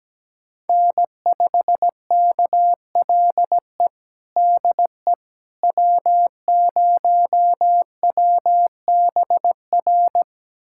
Morse code, heard as a keyed tone, about 17 wpm, 700 Hz. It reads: N5KLE DE W0WBR
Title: USA Callsign DE Callsign - 5 Characters - 17wpm